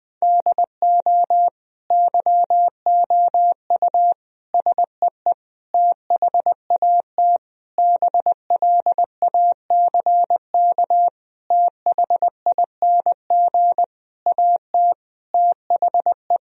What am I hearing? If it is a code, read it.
Morse code, heard as a keyed tone, about 20 wpm, 700 Hz. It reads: DO YOU SEE THAT BLACK THING AT THE
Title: Sentences from Top 1000 Words - Review Part 2 - 20wpm